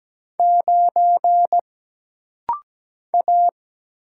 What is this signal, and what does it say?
Morse code, heard as a keyed tone, about 17 wpm, 700 Hz. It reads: 9 A
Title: Single Letter-Number - Rapid-Fire 17wpm